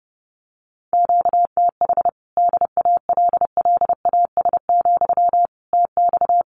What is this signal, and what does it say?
Morse code, heard as a keyed tone, about 30 wpm, 700 Hz. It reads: QTH DALLAS, TX
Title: Common QSO Phrases Encore 30wpm